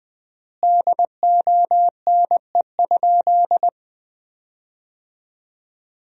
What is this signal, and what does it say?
Morse code, heard as a keyed tone, about 20 wpm, 700 Hz. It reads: DONE?
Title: 3x Word-Spacing - Sentences from Top 500 Words - 20wpm